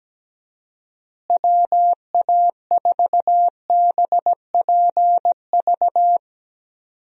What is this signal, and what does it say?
Morse code, heard as a keyed tone, about 17 wpm, 700 Hz. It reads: WA4BPV
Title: USA Call Signs - 6 Characters - Repeated 3x - 17wpm